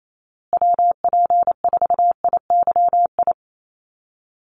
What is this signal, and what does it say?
Morse code, heard as a keyed tone, about 28 wpm, 700 Hz. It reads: WP4IYI
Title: USA Call Signs Repeated 3x 28wpm